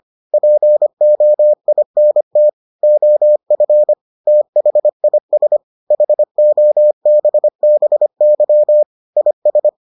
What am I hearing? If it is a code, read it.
Morse code, heard as a keyed tone, about 25 wpm, 600 Hz. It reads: POINT OF THIS HOBBY IS